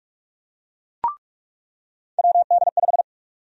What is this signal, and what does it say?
Morse code, heard as a keyed tone, about 46 wpm, 700 Hz. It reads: WD5
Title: USA Call Sign Prefixes - 45wpm